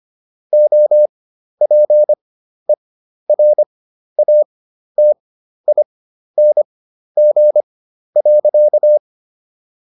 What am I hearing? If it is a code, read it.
Morse code, heard as a keyed tone, about 25 wpm, spaced as for 12 wpm, 600 Hz. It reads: OPERATING.